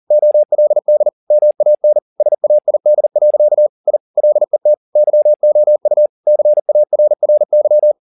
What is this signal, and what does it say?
Morse code, heard as a keyed tone, about 40 wpm, 600 Hz. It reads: OLD MAN SAID. I LET YOU CARRY